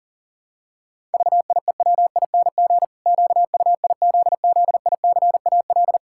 Morse code, heard as a keyed tone, about 40 wpm, 700 Hz. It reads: VIEWING QUIZZICAL
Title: Pangram Challenge v2 - 40wpm